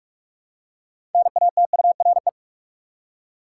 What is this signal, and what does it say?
Morse code, heard as a keyed tone, about 46 wpm, 700 Hz. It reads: NATURE
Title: Six Letter Words - 45wpm